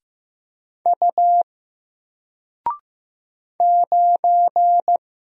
Morse code, heard as a keyed tone, about 15 wpm, 700 Hz. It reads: U 9
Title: Single Character (Letter:Number:Punctuation:Prosign) 15wpm